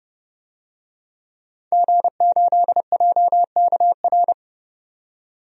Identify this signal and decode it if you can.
Morse code, heard as a keyed tone, about 30 wpm, 700 Hz. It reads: G8JKR
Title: DXCC-100 Call Signs - 5 Characters - 30wpm